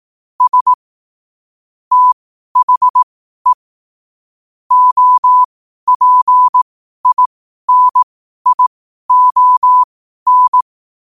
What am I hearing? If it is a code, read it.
Morse code, heard as a keyed tone, about 18 wpm, spaced as for 12 wpm, 1000 Hz. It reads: S THE OPINION
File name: qotdfiles / 12wpm